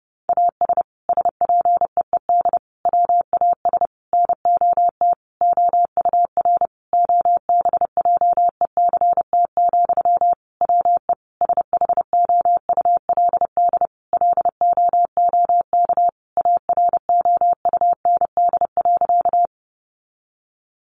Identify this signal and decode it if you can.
Morse code, heard as a keyed tone, about 30 wpm, 700 Hz. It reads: AS SPEED WAS NOT OUR OBJECT, WE SHOULD LOOK AROUND.
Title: Sentences from Top 800 Words 30wpm